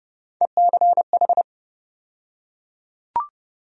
Morse code, heard as a keyed tone, about 30 wpm, 700 Hz. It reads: ECH
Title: USA Call Sign Postfixes - 3 Letters 30wpm